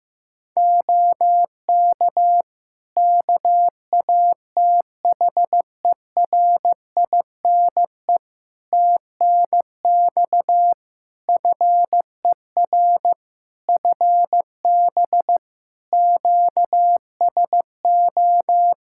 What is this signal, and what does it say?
Morse code, heard as a keyed tone, about 15 wpm, 700 Hz. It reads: OK KATHERINE TNX FER FB QSO
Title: K1USN SST QSO Practice 15wpm